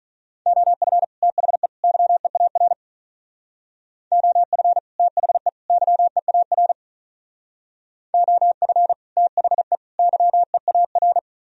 Morse code, heard as a keyed tone, about 43 wpm, 700 Hz. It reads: OF THE YEAR OF THE YEAR OF THE YEAR
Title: Speed-Racing - Sets of 3 Words 35wpm